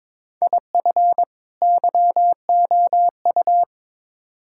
Morse code, heard as a keyed tone, about 22 wpm, 700 Hz. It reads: IF YOU
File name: practice / 22wpm / Build A Phrase - 5-Word Phrases - 22wpm